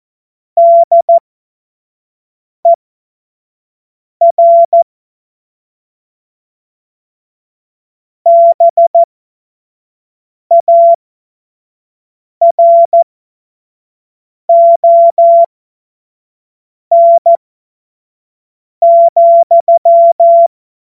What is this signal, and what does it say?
Morse code, heard as a keyed tone, about 14 wpm, spaced as for 5 wpm, 680 Hz. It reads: DER BARON,
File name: Morsekurs / Witze / Witz 16 Januar 2024 25 F70